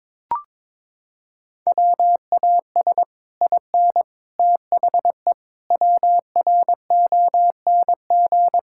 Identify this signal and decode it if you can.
Morse code, heard as a keyed tone, about 21 wpm, 700 Hz. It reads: WAS IN THE WRONG
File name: Sets of 9 Words - 22wpm